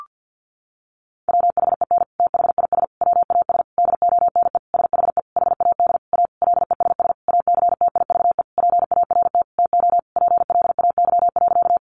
Morse code, heard as a keyed tone, about 50 wpm, 700 Hz. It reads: WHEN THIS WAS DONE SHE HAD A LESS ACTIVE PART TO PLAY.
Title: Sentences from Top 900 Words 50wpm